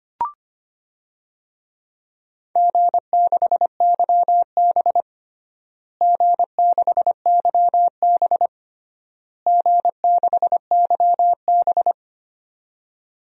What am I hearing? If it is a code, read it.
Morse code, heard as a keyed tone, about 25 wpm, 700 Hz. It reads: G6YB G6YB G6YB
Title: DXCC-100 Call Signs - 4 Characters - Repeated 3x - 25wpm